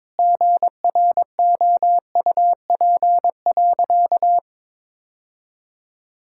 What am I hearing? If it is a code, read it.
Morse code, heard as a keyed tone, about 22 wpm, 700 Hz. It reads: GROUP.
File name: practice / 22wpm / Sentences with 2-Word Phrases - 22wpm